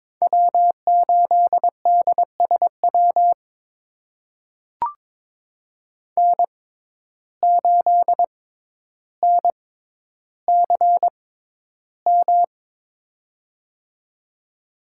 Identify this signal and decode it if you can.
Morse code, heard as a keyed tone, about 22 wpm, spaced as for 11 wpm, 700 Hz. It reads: W8DSW N8NCM
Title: USA Call Signs - 5 Characters - 6x Character Spacing - 22wpm